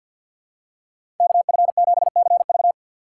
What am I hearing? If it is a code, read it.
Morse code, heard as a keyed tone, about 53 wpm, 700 Hz. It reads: KF7CV
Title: USA Call Signs - 5 Characters - 4x Character Spacing - 50wpm